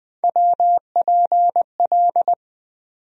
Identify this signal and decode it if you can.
Morse code, heard as a keyed tone, about 20 wpm, 700 Hz. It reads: WPL
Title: USA Call Sign Postfixes - 3 Letters 20wpm